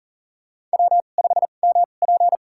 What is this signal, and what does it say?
Morse code, heard as a keyed tone, about 40 wpm, spaced as for 29 wpm, 700 Hz. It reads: W5MP
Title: USA Call Signs - 4 Characters - 2x Character Spacing - 40wpm